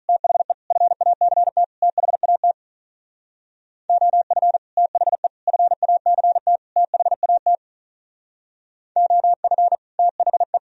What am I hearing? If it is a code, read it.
Morse code, heard as a keyed tone, about 43 wpm, 700 Hz. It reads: THE FACT THAT OF THE FACT THAT OF THE
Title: Speed-Racing - Sets of 4 Words 35wpm